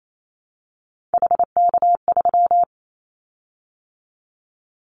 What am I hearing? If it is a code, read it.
Morse code, heard as a keyed tone, about 28 wpm, 700 Hz. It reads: HK3
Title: Callsign Prefixes for Top 100 DXCC - 28wpm